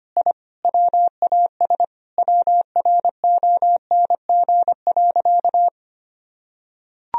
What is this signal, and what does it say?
Morse code, heard as a keyed tone, about 25 wpm, 700 Hz. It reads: I WAS WRONG.
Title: Sentences from Top 1000 Words - Review Part 3 - 25wpm